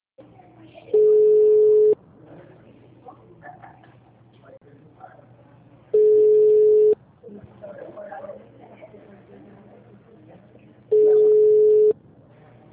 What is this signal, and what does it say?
Ringback tone (Europe, 425 Hz), with speech.